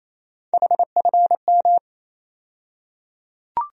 Morse code, heard as a keyed tone, about 28 wpm, 700 Hz. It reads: HFM
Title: USA Call Sign Postfixes - 3 Letters 28wpm